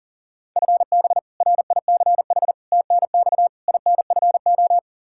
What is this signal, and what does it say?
Morse code, heard as a keyed tone, about 40 wpm, 700 Hz. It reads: FB RICH TNX INFO
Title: Practice QSO - Exchange 3 - More Info - 40wpm